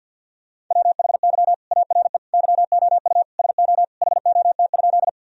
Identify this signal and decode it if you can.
Morse code, heard as a keyed tone, about 50 wpm, 700 Hz. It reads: WHY ARE YOU SO HOT?
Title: Sentences from Top 1000 Words - Review Part 3 - 50wpm